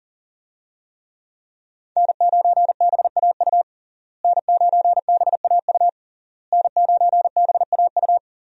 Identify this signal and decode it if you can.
Morse code, heard as a keyed tone, about 40 wpm, 700 Hz. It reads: N9BAU N9BAU N9BAU
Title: USA Call Signs - 5 Characters - Repeated 3x - 40wpm